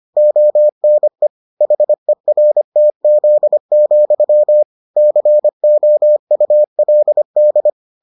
Morse code, heard as a keyed tone, about 25 wpm, 600 Hz. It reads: ONE HERTZ, COULD